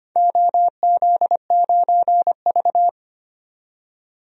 Morse code, heard as a keyed tone, about 25 wpm, 700 Hz. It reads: OZ9V
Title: DXCC-100 Call Signs - 4 Characters - 25wpm